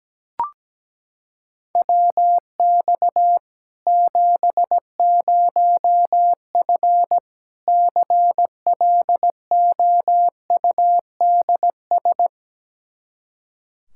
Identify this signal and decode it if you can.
Morse code, heard as a keyed tone, about 17 wpm, 700 Hz. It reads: WX 70F CLOUDS
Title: K1USN SST QSO Practice 17wpm